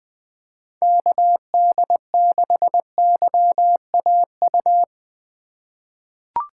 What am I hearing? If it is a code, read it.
Morse code, heard as a keyed tone, about 20 wpm, 700 Hz. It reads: KD6YAU
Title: USA Call Signs - 6 Characters - 4x Character Spacing - 20wpm